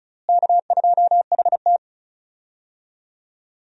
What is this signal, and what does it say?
Morse code, heard as a keyed tone, about 36 wpm, 700 Hz. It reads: K2HT
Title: USA Call Signs - 4 Characters - Repeated 3x - 35wpm